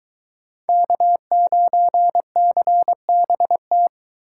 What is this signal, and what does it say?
Morse code, heard as a keyed tone, about 23 wpm, 700 Hz. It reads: K9CBT